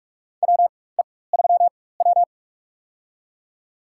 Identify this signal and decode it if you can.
Morse code, heard as a keyed tone, about 41 wpm, spaced as for 20 wpm, 700 Hz. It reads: WE3W
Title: USA Call Signs - 4 Characters - 4x Character Spacing - 45wpm